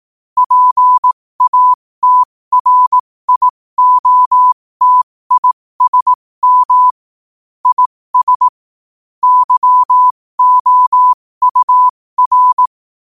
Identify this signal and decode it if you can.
Morse code, heard as a keyed tone, about 18 wpm, spaced as for 15 wpm, 1000 Hz. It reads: PATRIOTISM IS YOUR
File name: qotdfiles / 15wpm